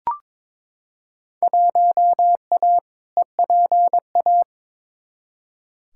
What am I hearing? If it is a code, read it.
Morse code, heard as a keyed tone, about 21 wpm, 700 Hz. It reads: 1A EPA
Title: Field Day Exchanges - 22wpm